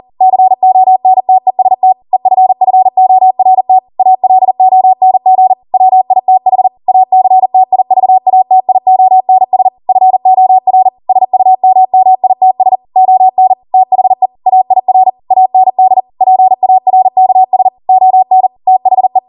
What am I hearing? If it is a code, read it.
Morse code, heard as a keyed tone, about 40 wpm, 750 Hz. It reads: CONTEST EFFORT ALONG WITH ACTIVATIONS FOR SUMMITS ON THE AIR AND PARKS ON THE